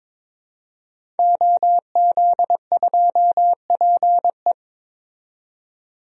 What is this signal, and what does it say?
Morse code, heard as a keyed tone, about 22 wpm, 700 Hz. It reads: OZ2PE
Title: DXCC-100 Call Signs - 5 Characters - 22wpm